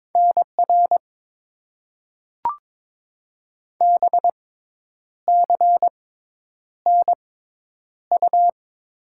Morse code, heard as a keyed tone, about 22 wpm, spaced as for 9 wpm, 700 Hz. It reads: NR BCNU
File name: Top 90 QSO Elements - 6x Character Spacing - 22wpm